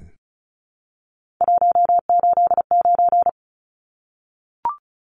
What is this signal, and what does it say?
Morse code, heard as a keyed tone, about 34 wpm, 700 Hz. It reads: 189
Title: Numbers - 3-Digit 35wpm